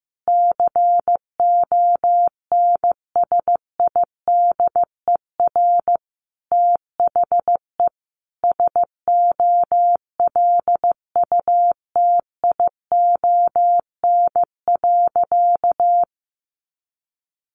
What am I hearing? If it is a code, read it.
Morse code, heard as a keyed tone, about 15 wpm, 700 Hz. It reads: CONSIDER THE SOLUTION.
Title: Sentences from Top 1300 Words 15wpm